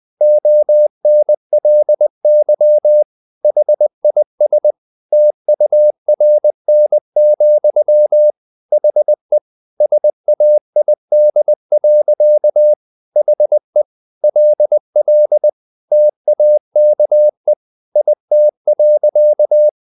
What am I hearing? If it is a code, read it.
Morse code, heard as a keyed tone, about 20 wpm, 600 Hz. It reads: ONLY HIS TURN, HE SAID. HE LL TAKE IT.
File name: the old man and the sea / the old man and the sea part 1 20wpm